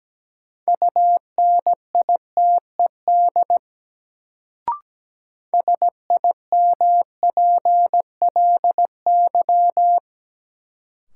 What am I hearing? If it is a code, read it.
Morse code, heard as a keyed tone, about 17 wpm, 700 Hz. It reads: UNITED SIMPLY
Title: Sentences from Top 600 Words - New Word Drill - 17wpm